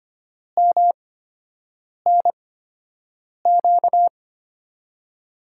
Morse code, heard as a keyed tone, about 25 wpm, spaced as for 7 wpm, 700 Hz. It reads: MNQ
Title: USA Call Signs - 5 Characters - 8x Character Spacing - 25wpm